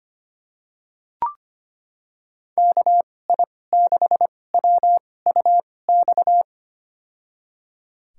Morse code, heard as a keyed tone, about 25 wpm, spaced as for 18 wpm, 700 Hz. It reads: KI6WUX
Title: USA Call Signs - 2x Character Spacing - 25wpm